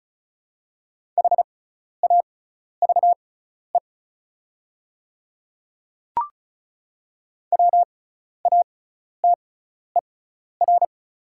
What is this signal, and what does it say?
Morse code, heard as a keyed tone, about 36 wpm, spaced as for 12 wpm, 690 Hz. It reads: HAVE WATER
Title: 6x Character Spacing - Top 100 Words - 35wpm